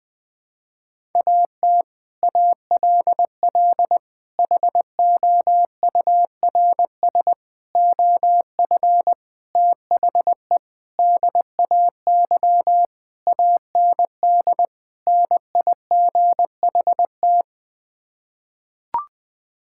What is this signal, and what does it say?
Morse code, heard as a keyed tone, about 21 wpm, 700 Hz. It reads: AT ALL HOURS OF THE DAY AND NIGHT
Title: Sets of 8 Words - 20wpm